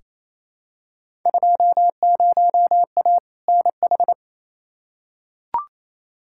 Morse code, heard as a keyed tone, about 27 wpm, 700 Hz. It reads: 20A NH
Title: Field Day Exchanges Encore - 28wpm